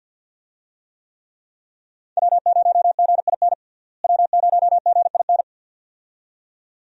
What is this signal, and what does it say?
Morse code, heard as a keyed tone, about 52 wpm, 700 Hz. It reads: W0GIN W0GIN
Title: USA Call Signs - 5 Characters - Repeated 2x - 50wpm